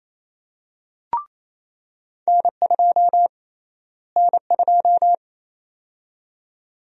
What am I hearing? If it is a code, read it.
Morse code, heard as a keyed tone, about 27 wpm, 700 Hz. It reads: N2 N2
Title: USA Call Sign Prefixes - Repeated 2x - 28wpm